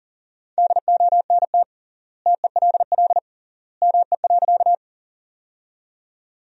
Morse code, heard as a keyed tone, about 40 wpm, 700 Hz. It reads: DONT TELL ME.
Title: 3x Word-Spacing - Sentences from Top 300 Words - 40wpm